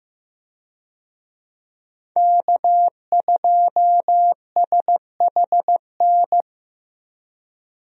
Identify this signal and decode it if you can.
Morse code, heard as a keyed tone, about 15 wpm, 700 Hz. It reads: K2SHN
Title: USA Call Signs - 5 Characters - Repeated 2x - 15wpm